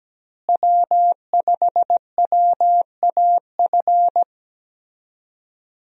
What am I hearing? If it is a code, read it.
Morse code, heard as a keyed tone, about 17 wpm, 700 Hz. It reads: W5WAF
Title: USA Call Signs - 5 Characters - Repeated 3x - 17wpm